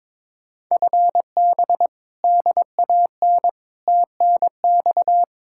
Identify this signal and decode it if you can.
Morse code, heard as a keyed tone, about 22 wpm, 700 Hz. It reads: FB DAN TNX
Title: Speed-Racing Common QSO Phrases - 22wpm